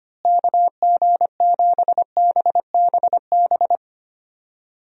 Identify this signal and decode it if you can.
Morse code, heard as a keyed tone, about 25 wpm, 700 Hz. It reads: KG7BBB
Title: USA Call Signs - 6 Characters - Repeated 2x - 25wpm